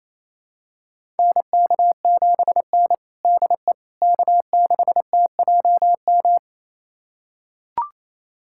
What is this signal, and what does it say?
Morse code, heard as a keyed tone, about 28 wpm, 700 Hz. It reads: NK7N DE K6TJM
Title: USA Callsign DE Callsign 28wpm